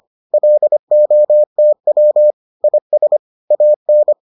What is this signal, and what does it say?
Morse code, heard as a keyed tone, about 25 wpm, 600 Hz. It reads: LOTW IS AN